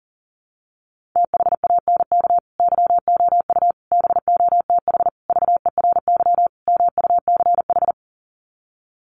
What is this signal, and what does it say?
Morse code, heard as a keyed tone, about 40 wpm, 700 Hz. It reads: THANK YOU BOTH VERY MUCH
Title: Sets of 5 Words 40wpm